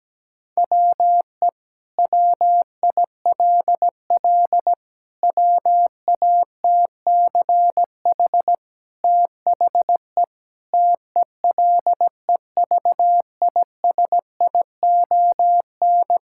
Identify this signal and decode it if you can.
Morse code, heard as a keyed tone, about 17 wpm, 700 Hz. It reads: WE WILL WATCH THE TELEVISION
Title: Sentences from Top 1000 Words - Review Part 5 - 17wpm